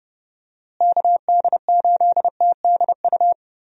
Morse code, heard as a keyed tone, about 30 wpm, 700 Hz. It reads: KD8TDU
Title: USA Call Signs - 6 Characters - Repeated 3x - 30wpm